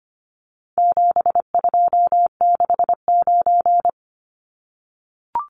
Morse code, heard as a keyed tone, about 25 wpm, 700 Hz. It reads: 7269
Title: Numbers - 4-Digit 25wpm